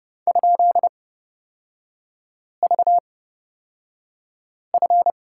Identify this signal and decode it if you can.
Morse code, heard as a keyed tone, about 30 wpm, spaced as for 5 wpm, 700 Hz. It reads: ?VF
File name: Lesson 37e - Letter x - Three Characters 30wpm